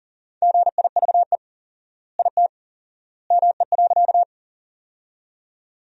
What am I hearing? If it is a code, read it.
Morse code, heard as a keyed tone, about 40 wpm, 700 Hz. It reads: GIVE IT ME.
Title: 4x Word-Spacing - Sentences from Top 200 Words - 40wpm